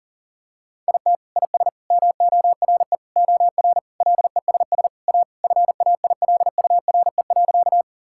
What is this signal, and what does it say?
Morse code, heard as a keyed tone, about 40 wpm, 700 Hz. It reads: IT IS MORE OR LESS A FAILURE.
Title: Sentences from Top 1000 Words - Review Part 6 - 40wpm